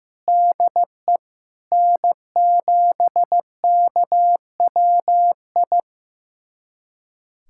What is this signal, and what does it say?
Morse code, heard as a keyed tone, about 15 wpm, 700 Hz. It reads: DE N7KWI
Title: USA Callsign DE Callsign 15wpm